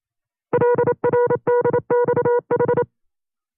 Morse code, heard as a keyed tone, about 27 wpm, 480 Hz. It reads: LRDXH
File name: CW-practice/25WPM